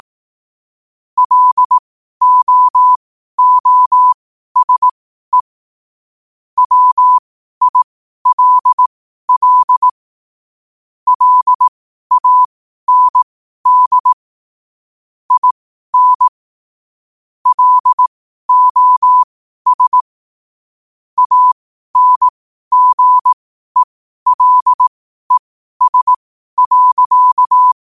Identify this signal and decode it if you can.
Morse code, heard as a keyed tone, about 18 wpm, spaced as for 12 wpm, 1000 Hz. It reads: LOOSE WILL LAND IN LOS ANGELES.
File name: qotdfiles / 12wpm